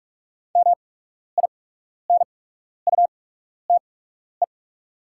Morse code, heard as a keyed tone, about 44 wpm, spaced as for 12 wpm, 690 Hz. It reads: MINUTE
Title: 8x Character Spacing - Top 500 Words - 45wpm